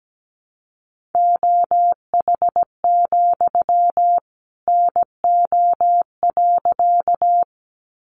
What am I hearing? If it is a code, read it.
Morse code, heard as a keyed tone, about 17 wpm, 700 Hz. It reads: OH, NO.
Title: Sentences from Top 500 Words 17wpm